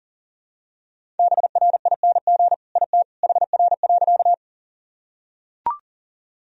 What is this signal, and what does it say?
Morse code, heard as a keyed tone, about 40 wpm, 700 Hz. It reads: BRING IT HR.
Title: Sentences from Top 500 Words and Abbreviations 40wpm